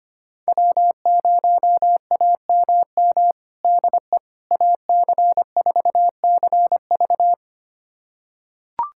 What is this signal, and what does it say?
Morse code, heard as a keyed tone, about 25 wpm, 700 Hz. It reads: W0AMM DE AC4CV
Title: USA Callsign DE Callsign 25wpm